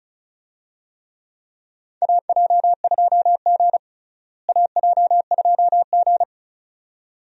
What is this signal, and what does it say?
Morse code, heard as a keyed tone, about 36 wpm, 700 Hz. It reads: AJ2G AJ2G
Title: USA Call Signs - 4 Characters - Repeated 2x - 35wpm